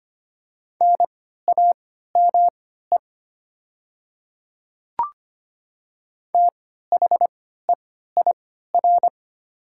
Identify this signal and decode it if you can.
Morse code, heard as a keyed tone, about 25 wpm, spaced as for 14 wpm, 700 Hz. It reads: NAME THEIR
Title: 3x Character Spacing - Top 200 Words - 25wpm